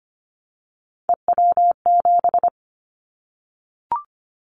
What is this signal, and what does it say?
Morse code, heard as a keyed tone, about 25 wpm, 700 Hz. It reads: EW7
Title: Callsign Prefixes for Top 50 DXCC - 25wpm